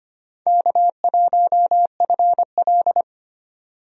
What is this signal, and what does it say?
Morse code, heard as a keyed tone, about 25 wpm, 700 Hz. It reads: K1FL
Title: USA Call Signs - 4 Characters - 25wpm